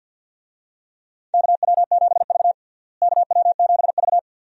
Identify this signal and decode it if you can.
Morse code, heard as a keyed tone, about 50 wpm, 700 Hz. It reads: KW7V KW7V
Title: USA Call Signs - 4 Characters - Repeated 2x - 50wpm